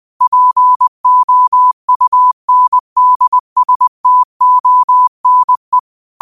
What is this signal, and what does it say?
Morse code, heard as a keyed tone, about 20 wpm, 1000 Hz. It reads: POUNDSTONE